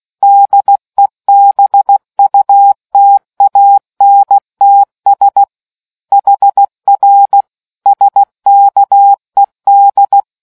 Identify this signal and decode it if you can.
Morse code, heard as a keyed tone, about 16 wpm, 800 Hz. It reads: DEBUTANTS HR SKED